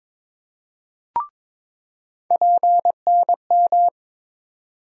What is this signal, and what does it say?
Morse code, heard as a keyed tone, about 21 wpm, 700 Hz. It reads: PNM
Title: USA Call Sign Postfixes - 3 Letters 22wpm